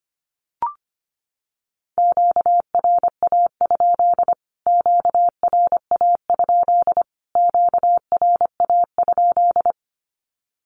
Morse code, heard as a keyed tone, about 25 wpm, 700 Hz. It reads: QRA? QRA? QRA?
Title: Q-Codes Repeated 3x 25wpm